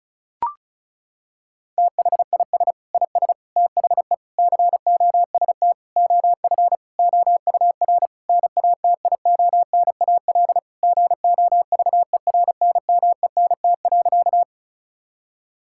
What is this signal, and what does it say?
Morse code, heard as a keyed tone, about 34 wpm, 700 Hz. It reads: THIS IS THE COST OF OUR NATIONAL GOVERNMENT.
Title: Sentences from Top 1000 Words - Review Part 5 - 35wpm